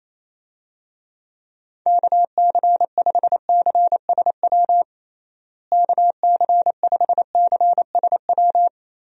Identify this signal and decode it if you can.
Morse code, heard as a keyed tone, about 28 wpm, 700 Hz. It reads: KC5CSW KC5CSW
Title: USA Call Signs - 6 Characters - Repeated 3x - 28wpm